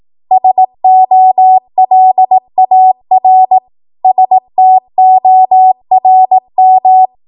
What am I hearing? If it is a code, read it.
Morse code, heard as a keyed tone, about 18 wpm, 750 Hz. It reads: SOLAR STORM
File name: Archive / 18 WPM / 18WPM